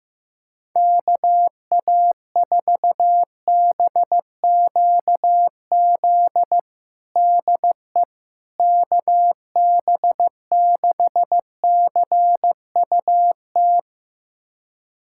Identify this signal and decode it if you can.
Morse code, heard as a keyed tone, about 15 wpm, 700 Hz. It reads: KA4BQZ DE KB6CUT
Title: USA Callsign DE Callsign - 6 Characters - 15wpm